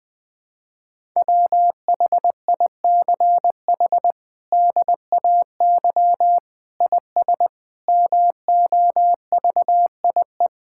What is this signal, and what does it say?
Morse code, heard as a keyed tone, about 20 wpm, 700 Hz. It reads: WHICH DAY IS MOVIE
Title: Sentences from Top 1000 Words - Review Part 5 - 20wpm